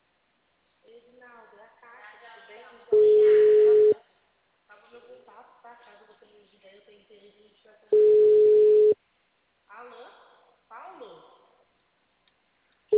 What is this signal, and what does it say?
Ringback tone (Europe, 425 Hz), with speech.
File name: out- -2013- - - .79638